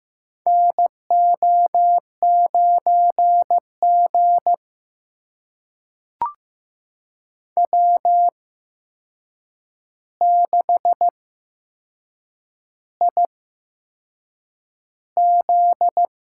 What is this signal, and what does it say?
Morse code, heard as a keyed tone, about 15 wpm, spaced as for 6 wpm, 700 Hz. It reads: NO9G W6IZ
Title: USA Call Signs - 8x Character Spacing - 15wpm